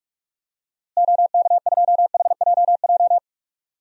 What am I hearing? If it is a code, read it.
Morse code, heard as a keyed tone, about 46 wpm, 700 Hz. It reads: OK2HJJ
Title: DXCC-100 Call Signs - 6 Characters - Repeated 2x - 45wpm